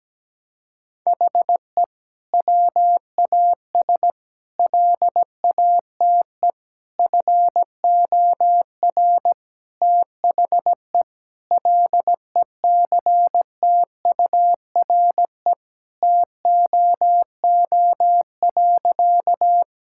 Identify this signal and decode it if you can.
Morse code, heard as a keyed tone, about 17 wpm, 700 Hz. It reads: HE WAS LATE FOR THE LECTURE TOO.
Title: Sentences from Top 1600 Words 17wpm